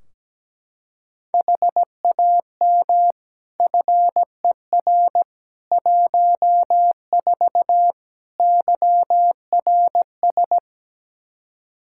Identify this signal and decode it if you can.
Morse code, heard as a keyed tone, about 17 wpm, 700 Hz. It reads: HAM FER 14 YRS
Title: K1USN SST QSO Practice 17wpm